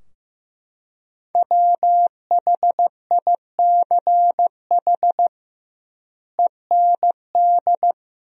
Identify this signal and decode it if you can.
Morse code, heard as a keyed tone, about 15 wpm, 700 Hz. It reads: WHICH END
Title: 2x Word-Spacing - Sentences from Top 300 Words - 15wpm